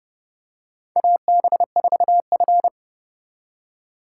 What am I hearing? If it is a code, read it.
Morse code, heard as a keyed tone, about 30 wpm, 700 Hz. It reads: AB4F
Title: USA Call Signs - 4 Characters - 4x Character Spacing - 30wpm